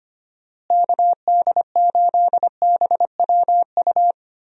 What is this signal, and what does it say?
Morse code, heard as a keyed tone, about 25 wpm, 700 Hz. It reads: KD8BWU
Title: USA Call Signs - 6 Characters - Repeated 2x - 25wpm